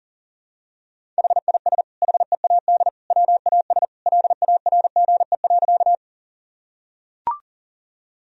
Morse code, heard as a keyed tone, about 40 wpm, 700 Hz. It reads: HIS HEAD WAS LARGE.
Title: Sentences from Top 1000 Words - Review Part 1 - 40wpm